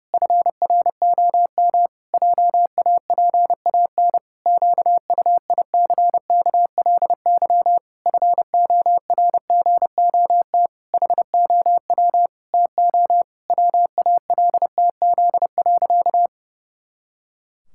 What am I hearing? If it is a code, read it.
Morse code, heard as a keyed tone, about 30 wpm, 700 Hz. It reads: FROM JAPAN QUICKLY FORGOT HOW TO WALTZ.
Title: Pangram Challenge v2 - 30wpm